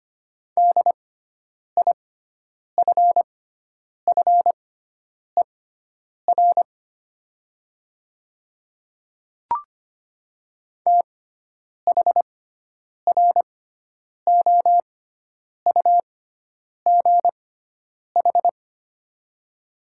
Morse code, heard as a keyed tone, about 25 wpm, spaced as for 9 wpm, 700 Hz. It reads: DIFFER THROUGH
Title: 6x Character Spacing - Top 300 Words - 25wpm